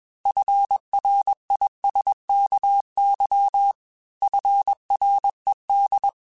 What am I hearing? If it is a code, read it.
Morse code, heard as a keyed tone, about 21 wpm, 800 Hz. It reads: FRISKY FRED
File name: F Runs / F Runs mp3s / F Runs @20